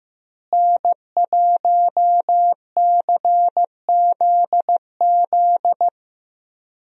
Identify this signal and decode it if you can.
Morse code, heard as a keyed tone, about 15 wpm, 700 Hz. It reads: N1CZZ
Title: USA Call Signs - 5 Characters - Repeated 3x - 15wpm